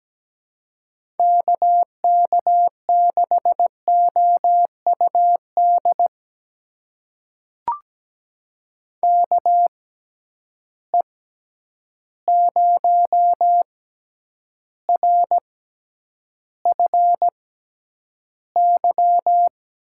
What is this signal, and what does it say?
Morse code, heard as a keyed tone, about 17 wpm, spaced as for 9 wpm, 700 Hz. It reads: KK6OUD KE0RFY
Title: USA Call Signs - 6 Characters - 6x Character Spacing - 17wpm